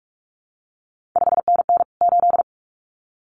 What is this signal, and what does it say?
Morse code, heard as a keyed tone, about 46 wpm, 700 Hz. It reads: 5NN 8
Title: CQ WW DX Contest Exchange Encore - 45wpm